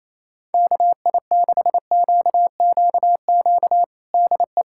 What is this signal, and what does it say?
Morse code, heard as a keyed tone, about 28 wpm, 700 Hz. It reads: KI6QQQ DE
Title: USA Callsign DE Callsign 28wpm